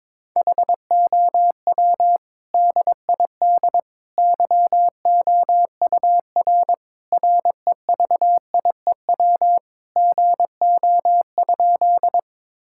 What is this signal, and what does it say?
Morse code, heard as a keyed tone, about 22 wpm, 700 Hz. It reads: HOW DID YOUR REVIEW GO?
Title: Sentences from Top 1000 Words - Review Part 6 - 22wpm